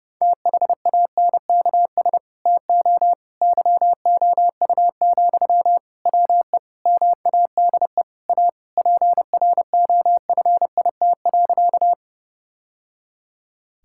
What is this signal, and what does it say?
Morse code, heard as a keyed tone, about 30 wpm, 700 Hz. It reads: THANKS TO YOU, WE MADE A PROFIT.
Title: Sentences from Top 1000 Words - Review Part 5 - 30wpm